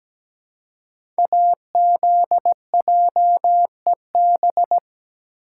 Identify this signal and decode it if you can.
Morse code, heard as a keyed tone, about 17 wpm, 700 Hz. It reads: AZJEB
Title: Code Groups - 5 Letter:Numbers - 17wpm